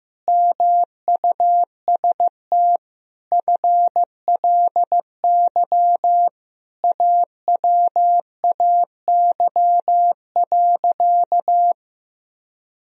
Morse code, heard as a keyed tone, about 15 wpm, 700 Hz. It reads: MUST FLY AWAY.
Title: Sentences from Top 1000 Words - Review Part 2 - 15wpm